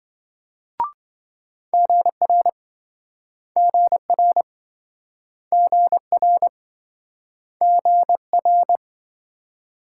Morse code, heard as a keyed tone, about 27 wpm, 700 Hz. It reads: GR GR GR GR